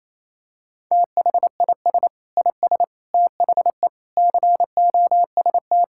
Morse code, heard as a keyed tone, about 28 wpm, 700 Hz. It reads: THIS IS THE COST
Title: Sentences from Top 1000 Words - Review Part 5 - 28wpm